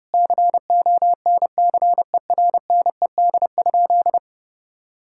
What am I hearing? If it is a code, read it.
Morse code, heard as a keyed tone, about 30 wpm, 700 Hz. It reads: CONCERNED?
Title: Sentences from Top 1000 Words - Review Part 6 - 30wpm